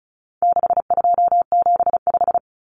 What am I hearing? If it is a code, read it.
Morse code, heard as a keyed tone, about 34 wpm, 700 Hz. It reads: 6275
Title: Common Cut Numbers - 4-Digit 35wpm